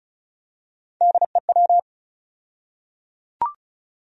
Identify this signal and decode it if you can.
Morse code, heard as a keyed tone, about 34 wpm, 700 Hz. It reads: DEW
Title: USA Call Sign Postfixes - 3 Letters 35wpm